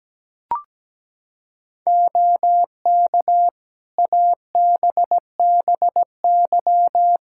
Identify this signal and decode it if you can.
Morse code, heard as a keyed tone, about 17 wpm, 700 Hz. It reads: OK ABBY
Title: Practice QSO - Exchange 4 - Ending - 17wpm